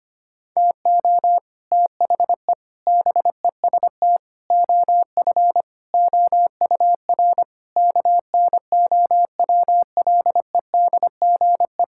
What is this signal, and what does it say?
Morse code, heard as a keyed tone, about 25 wpm, 700 Hz. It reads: TO THE BEST OF OUR KNOWLEDGE
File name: Sets of 9 Words - 25wpm